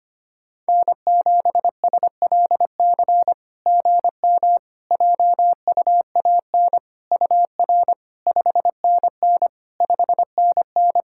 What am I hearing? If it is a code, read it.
Morse code, heard as a keyed tone, about 25 wpm, 690 Hz. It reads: N7SLC GM JUAN UR 5NN 5NN
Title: SOTA QSO Practice 25wpm